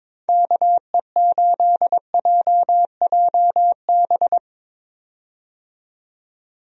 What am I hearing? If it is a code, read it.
Morse code, heard as a keyed tone, about 22 wpm, 700 Hz. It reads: KE8JJB
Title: USA Call Signs - 6 Characters - 22wpm